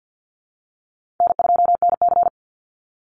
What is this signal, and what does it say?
Morse code, heard as a keyed tone, about 50 wpm, 700 Hz. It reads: N2NC
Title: Call Signs Repeated 2x - 50wpm